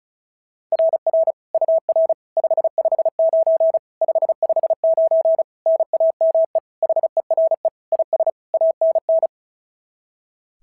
Morse code, heard as a keyed tone, about 36 wpm, 690 Hz. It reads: RR UR 559 559 NAME HERE IS ANN
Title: SOTA QSO Practice 35wpm